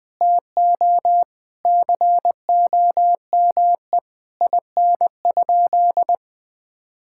Morse code, heard as a keyed tone, about 20 wpm, 700 Hz. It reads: TO COME IN?
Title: Sentences from Top 1600 Words 20wpm